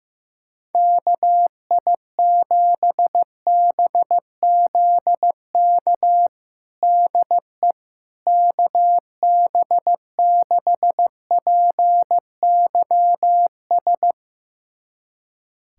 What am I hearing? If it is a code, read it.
Morse code, heard as a keyed tone, about 15 wpm, 700 Hz. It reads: KI7BZK DE KB6PYS
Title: USA Callsign DE Callsign - 6 Characters - 15wpm